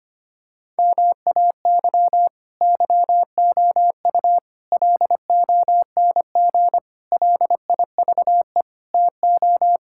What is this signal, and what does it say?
Morse code, heard as a keyed tone, about 25 wpm, 700 Hz. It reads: MAY YOU LONG LIVE TO
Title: Sentences from Top 1000 Words - Review Part 6 - 25wpm